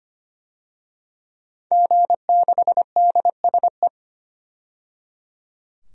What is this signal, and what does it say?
Morse code, heard as a keyed tone, about 25 wpm, 700 Hz. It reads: G6DSE
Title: DXCC-100 Call Signs - 5 Characters - 25wpm